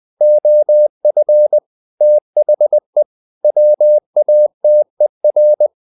Morse code, heard as a keyed tone, about 20 wpm, 600 Hz. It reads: OF THE WATER
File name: the old man and the sea part 2 20wpm